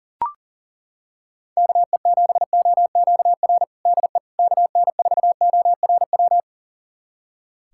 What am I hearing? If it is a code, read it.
Morse code, heard as a keyed tone, about 40 wpm, 700 Hz. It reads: KE7OQR DE KN4ORW
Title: USA Callsign DE Callsign - 6 Characters - 40wpm